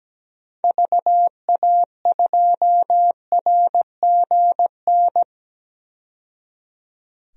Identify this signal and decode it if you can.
Morse code, heard as a keyed tone, about 17 wpm, 700 Hz. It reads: VA2RGN
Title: DXCC-100 Call Signs - 6 Characters - 17wpm